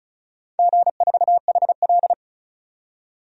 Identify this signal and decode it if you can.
Morse code, heard as a keyed tone, about 36 wpm, 700 Hz. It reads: G4HL
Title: DXCC-100 Call Signs - 4 Characters - 35wpm